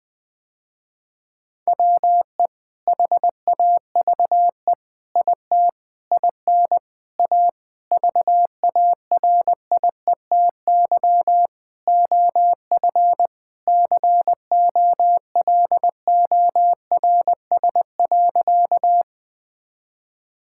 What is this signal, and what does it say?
Morse code, heard as a keyed tone, about 21 wpm, 700 Hz. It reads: WE HAVE IT IN A VARIETY OF COLORS.
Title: Sentences with 4-Word Phrases - Spoken Beginning - 20wpm